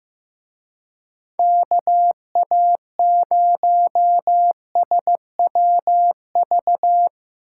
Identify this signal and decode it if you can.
Morse code, heard as a keyed tone, about 16 wpm, 700 Hz. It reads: KA0SWV
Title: Speed Racing - USA 2x3 Call Signs 15wpm